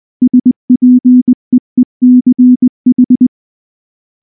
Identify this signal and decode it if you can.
Morse code, heard as a keyed tone, about 20 wpm, 250 Hz. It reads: SPEECH
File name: morse-training-20WPM